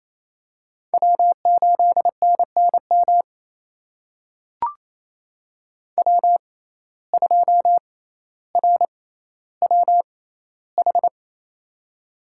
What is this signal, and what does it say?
Morse code, heard as a keyed tone, about 28 wpm, spaced as for 14 wpm, 700 Hz. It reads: W8NNM W2RWH
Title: USA Call Signs - 6x Character Spacing - 28wpm